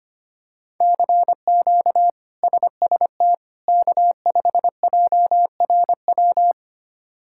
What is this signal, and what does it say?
Morse code, heard as a keyed tone, about 25 wpm, 700 Hz. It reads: CQ SST K5JRW
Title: K1USN SST QSO Practice - 25wpm